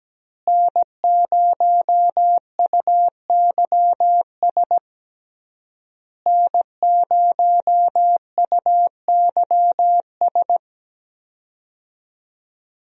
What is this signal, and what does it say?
Morse code, heard as a keyed tone, about 17 wpm, 700 Hz. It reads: N0UYS N0UYS
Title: USA Call Signs - 5 Characters - Repeated 3x - 17wpm